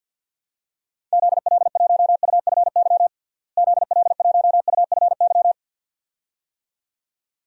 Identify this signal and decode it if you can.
Morse code, heard as a keyed tone, about 50 wpm, 700 Hz. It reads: ZL1UFY ZL1UFY
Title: DXCC-100 Call Signs - 6 Characters - Repeated 2x - 50wpm